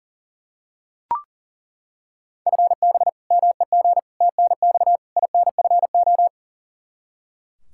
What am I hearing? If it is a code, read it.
Morse code, heard as a keyed tone, about 40 wpm, 700 Hz. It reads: FB MEG TNX INFO
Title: Practice QSO - Exchange 3 - More Info - 40wpm